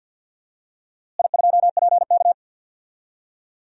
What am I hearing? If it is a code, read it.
Morse code, heard as a keyed tone, about 50 wpm, 700 Hz. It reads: I2PK
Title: DXCC-100 Call Signs - 4 Characters - Repeated 3x - 50wpm